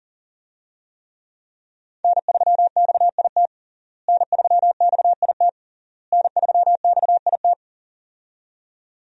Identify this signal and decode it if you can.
Morse code, heard as a keyed tone, about 40 wpm, 690 Hz. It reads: N3XIT N3XIT N3XIT
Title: USA Call Signs - 5 Characters - Repeated 3x - 40wpm